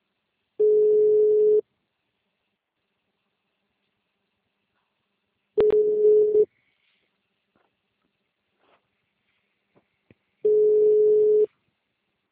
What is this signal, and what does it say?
Ringback tone (Europe, 425 Hz).
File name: out- -2039- - - .15165